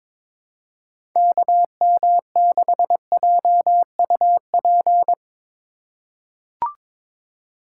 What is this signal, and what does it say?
Morse code, heard as a keyed tone, about 22 wpm, 700 Hz. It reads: KM6JUP
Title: USA Call Signs - 6 Characters - 3x Character Spacing - 22wpm